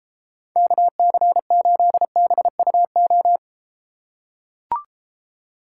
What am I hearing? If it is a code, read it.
Morse code, heard as a keyed tone, about 33 wpm, 700 Hz. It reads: KC8BUO